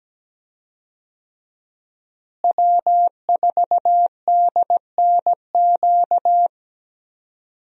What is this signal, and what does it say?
Morse code, heard as a keyed tone, about 17 wpm, 700 Hz. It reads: W4DNQ
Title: USA Call Signs - 5 Characters - Repeated 3x - 17wpm